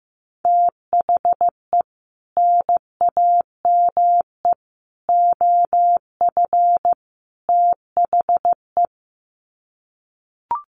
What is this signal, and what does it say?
Morse code, heard as a keyed tone, about 15 wpm, 700 Hz. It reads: THE NAME OF THE
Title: Sets of 4 Words 15wpm